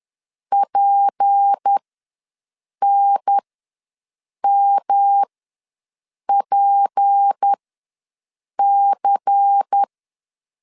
Morse code, heard as a keyed tone, about 10 wpm, spaced as for 6 wpm, 800 Hz. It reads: PNMPC
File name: LEZIONI / 6 PKX0 / 6 P ws 10wpm 3751